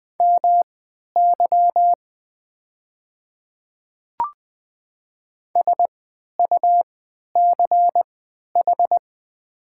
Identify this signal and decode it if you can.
Morse code, heard as a keyed tone, about 20 wpm, spaced as for 11 wpm, 700 Hz. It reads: MY SUCH
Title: 3x Character Spacing - Top 200 Words - 20wpm